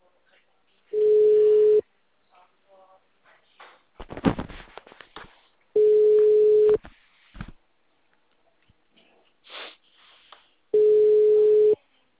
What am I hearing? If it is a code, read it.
Ringback tone (Europe, 425 Hz), with speech.
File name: out- -2002- - - .4